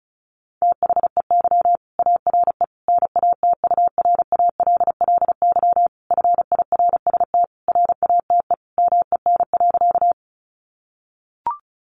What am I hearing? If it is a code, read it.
Morse code, heard as a keyed tone, about 34 wpm, 700 Hz. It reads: THEY ARE NATURALLY FIRST RATE MEN.
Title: Sentences from Top 1300 Words 35wpm